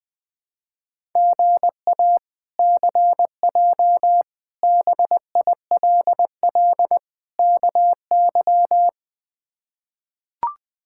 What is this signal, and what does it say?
Morse code, heard as a keyed tone, about 20 wpm, 700 Hz. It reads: GA CJ BILL KY
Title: K1USN SST QSO Practice - 20wpm